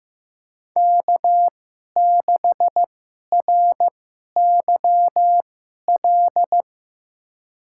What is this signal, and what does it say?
Morse code, heard as a keyed tone, about 15 wpm, spaced as for 11 wpm, 700 Hz. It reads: K6RYL
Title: USA Call Signs - 5 Characters - 2x Character Spacing - 15wpm